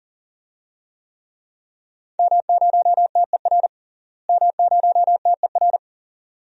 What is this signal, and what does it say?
Morse code, heard as a keyed tone, about 40 wpm, 700 Hz. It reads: M0TER M0TER
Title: DXCC-100 Call Signs - 5 Characters - Repeated 2x - 40wpm